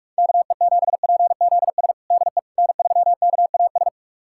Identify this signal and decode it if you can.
Morse code, heard as a keyed tone, about 46 wpm, 700 Hz. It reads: KE7PZS DE N3KAS
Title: USA Callsign DE Callsign 45wpm